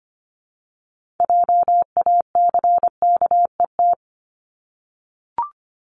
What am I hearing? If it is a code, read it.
Morse code, heard as a keyed tone, about 25 wpm, 700 Hz. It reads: JACKET
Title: Multi-letter Phonemes Repeated 3x 25wpm